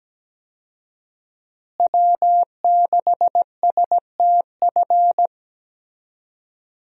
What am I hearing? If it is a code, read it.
Morse code, heard as a keyed tone, about 17 wpm, 700 Hz. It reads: W6STF
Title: USA Call Signs - 5 Characters - Repeated 2x - 17wpm